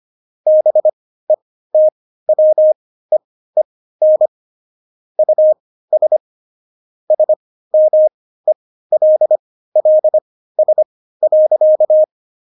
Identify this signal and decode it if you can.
Morse code, heard as a keyed tone, about 25 wpm, spaced as for 15 wpm, 620 Hz. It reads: BETWEEN US SMELLS.
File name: Jokes-25-char-15-eff